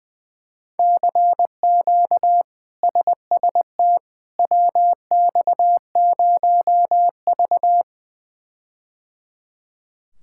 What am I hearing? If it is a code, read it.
Morse code, heard as a keyed tone, about 20 wpm, 700 Hz. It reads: CQ SST WX0V
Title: K1USN SST QSO Practice - 20wpm